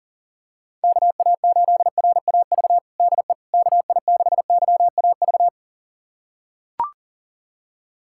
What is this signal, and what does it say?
Morse code, heard as a keyed tone, about 37 wpm, 700 Hz. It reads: KA8RAV DE KI6YAV
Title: USA Callsign DE Callsign - 6 Characters - 40wpm